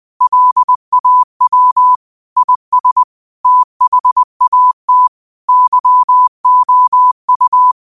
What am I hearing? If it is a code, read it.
Morse code, heard as a keyed tone, about 20 wpm, 1000 Hz. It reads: LAW IS THAT YOU